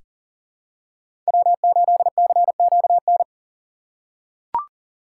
Morse code, heard as a keyed tone, about 40 wpm, 700 Hz. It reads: W8CQN
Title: USA Call Signs - 5 Characters - 3x Character Spacing - 40wpm